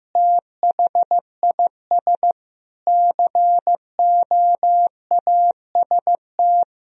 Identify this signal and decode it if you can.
Morse code, heard as a keyed tone, about 15 wpm, 700 Hz. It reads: THIS COAST
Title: Sentences from Top 1000 Words - Review Part 4 - 15wpm